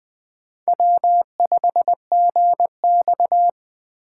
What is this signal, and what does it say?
Morse code, heard as a keyed tone, about 20 wpm, 700 Hz. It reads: W5GX
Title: USA Call Signs - 4 Characters - 3x Character Spacing - 20wpm